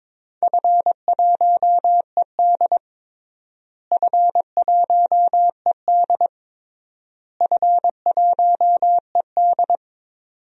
Morse code, heard as a keyed tone, about 21 wpm, 700 Hz. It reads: F1ED F1ED F1ED
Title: DXCC-100 Call Signs - 4 Characters - Repeated 3x - 22wpm